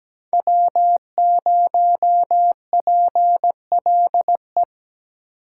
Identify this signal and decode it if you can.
Morse code, heard as a keyed tone, about 20 wpm, 700 Hz. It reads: W0PLE
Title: Speed Racing - USA 1x3 Call Signs 17wpm